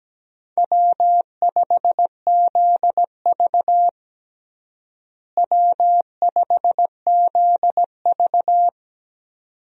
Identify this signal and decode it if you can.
Morse code, heard as a keyed tone, about 17 wpm, 700 Hz. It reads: W5ZV W5ZV
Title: USA Call Signs - 4 Characters - Repeated 2x - 17wpm